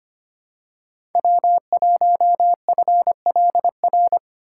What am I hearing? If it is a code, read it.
Morse code, heard as a keyed tone, about 25 wpm, 700 Hz. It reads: W1FLR
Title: USA Call Signs - 5 Characters - 2x Character Spacing - 25wpm